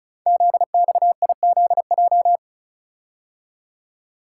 Morse code, heard as a keyed tone, about 36 wpm, 690 Hz. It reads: ZXIZJ
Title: Code Groups - 5 Letters - 35wpm